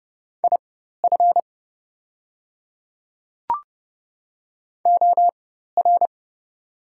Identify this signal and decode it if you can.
Morse code, heard as a keyed tone, about 30 wpm, spaced as for 14 wpm, 700 Hz. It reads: IF OR
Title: 4x Character Spacing - Top 100 Words - 30wpm